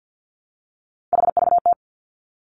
Morse code, heard as a keyed tone, about 50 wpm, 700 Hz. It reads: HVT
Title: Three Challenging Letter-Number Combinations 50wpm